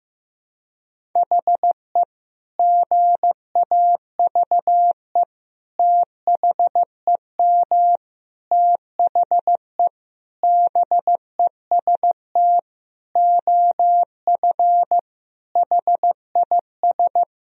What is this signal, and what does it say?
Morse code, heard as a keyed tone, about 15 wpm, 700 Hz. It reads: HE GAVE THEM THE BEST OF HIS
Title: Sentences from Top 1000 Words - Review Part 1 - 15wpm